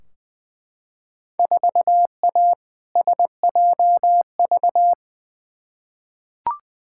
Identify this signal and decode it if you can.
Morse code, heard as a keyed tone, about 20 wpm, 700 Hz. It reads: 4A SJV
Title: Field Day Exchanges Encore - 20wpm